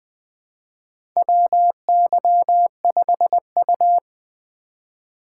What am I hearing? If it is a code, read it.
Morse code, heard as a keyed tone, about 20 wpm, 700 Hz. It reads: WY5U
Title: USA Call Signs - 4 Characters - 3x Character Spacing - 20wpm